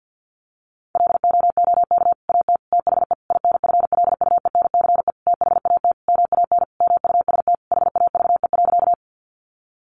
Morse code, heard as a keyed tone, about 50 wpm, 700 Hz. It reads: LOOK AT THE INFLUENCE THAT MAN MUST HAVE.
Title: Sentences from Top 1100 Words 50wpm